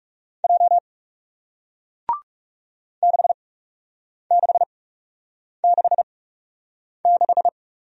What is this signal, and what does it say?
Morse code, heard as a keyed tone, about 39 wpm, spaced as for 8 wpm, 700 Hz. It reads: J 6666